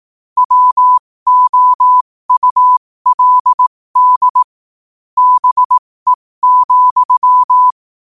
Morse code, heard as a keyed tone, about 18 wpm, spaced as for 15 wpm, 1000 Hz. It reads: WOULD BE,
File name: qotdfiles / 15wpm